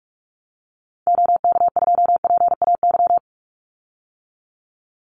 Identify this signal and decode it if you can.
Morse code, heard as a keyed tone, about 46 wpm, 700 Hz. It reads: OK2PAY
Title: Call Signs - 45wpm